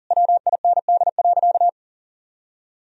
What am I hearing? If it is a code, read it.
Morse code, heard as a keyed tone, about 40 wpm, 700 Hz. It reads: WIND.
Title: Sentences from Top 1000 Words - Review Part 2 - 40wpm